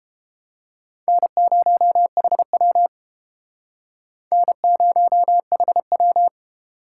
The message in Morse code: N0HW N0HW

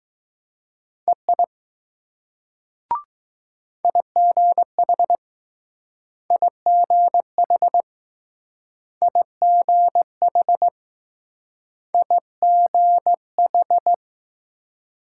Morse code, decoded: EI IGH IGH IGH IGH

